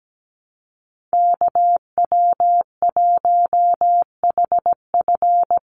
Morse code, heard as a keyed tone, about 17 wpm, 700 Hz. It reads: KW1HF
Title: USA Call Signs 17wpm